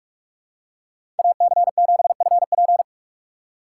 Morse code, heard as a keyed tone, about 48 wpm, 690 Hz. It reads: AC7FP
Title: USA Call Signs - 5 Characters - 2x Character Spacing - 45wpm